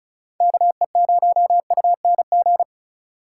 Morse code, heard as a keyed tone, about 36 wpm, 700 Hz. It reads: KE0UNG
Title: USA Call Signs - 6 Characters - 4x Character Spacing - 35wpm